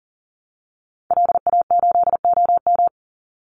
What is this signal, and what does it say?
Morse code, heard as a keyed tone, about 40 wpm, 700 Hz. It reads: LA8OM
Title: Call Signs - 40wpm